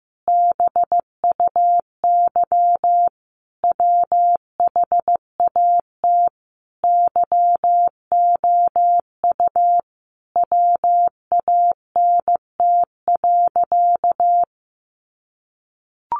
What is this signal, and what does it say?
Morse code, heard as a keyed tone, about 15 wpm, 700 Hz. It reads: BUY WHAT YOU WANT.
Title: Sentences from Top 1100 Words 15wpm